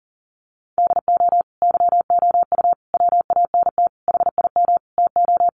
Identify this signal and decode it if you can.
Morse code, heard as a keyed tone, about 40 wpm, 700 Hz. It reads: DO YOU WANT HIM TO
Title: Sentences from Top 1400 Words 40wpm